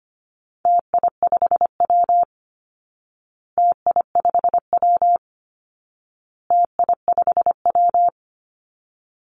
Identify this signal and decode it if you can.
Morse code, heard as a keyed tone, about 25 wpm, 700 Hz. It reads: TI5W TI5W TI5W
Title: Call Signs Repeated 2x - 25wpm